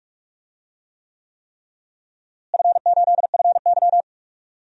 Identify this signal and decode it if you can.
Morse code, heard as a keyed tone, about 46 wpm, 700 Hz. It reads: F8FY
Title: DXCC-100 Call Signs - 4 Characters - Repeated 3x - 45wpm